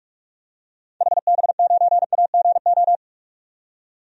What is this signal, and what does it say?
Morse code, heard as a keyed tone, about 41 wpm, 700 Hz. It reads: HB9AGO
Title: DXCC-100 Call Signs - 6 Characters - 45wpm